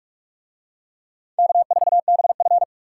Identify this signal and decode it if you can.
Morse code, heard as a keyed tone, about 43 wpm, 700 Hz. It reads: K4BF